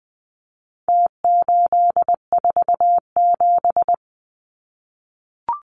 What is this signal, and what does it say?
Morse code, heard as a keyed tone, about 20 wpm, 700 Hz. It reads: T847
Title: Numbers - Contest Style - 4-Digit 20wpm